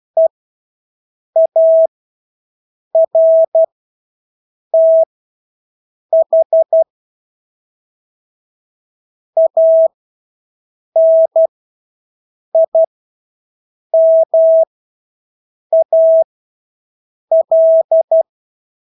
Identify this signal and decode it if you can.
Morse code, heard as a keyed tone, about 12 wpm, spaced as for 6 wpm, 650 Hz. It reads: EARTH ANIMAL
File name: Engelske ord 12-6 wpm